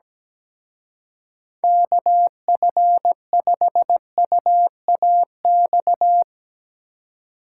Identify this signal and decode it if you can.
Morse code, heard as a keyed tone, about 18 wpm, 700 Hz. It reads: KF5UAX
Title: Speed Racing - USA 2x3 Call Signs 17wpm